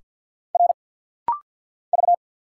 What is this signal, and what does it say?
Morse code, heard as a keyed tone, about 50 wpm, 690 Hz. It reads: R V
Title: Single Letters - Rapid-Fire 50wpm